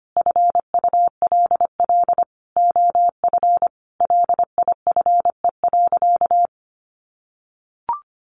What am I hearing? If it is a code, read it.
Morse code, heard as a keyed tone, about 25 wpm, 700 Hz. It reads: FULL OF LIFE.
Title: Sentences from Top 1300 Words 25wpm